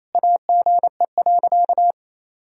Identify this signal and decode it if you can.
Morse code, heard as a keyed tone, about 28 wpm, 700 Hz. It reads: AGE.
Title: Sentences from Top 1000 Words - Review Part 3 - 28wpm